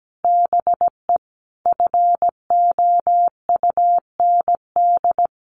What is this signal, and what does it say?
Morse code, heard as a keyed tone, about 17 wpm, 700 Hz. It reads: BE FOUND
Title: Sentences from Top 1400 Words 17wpm